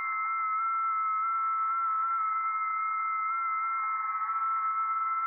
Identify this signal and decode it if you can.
FT8.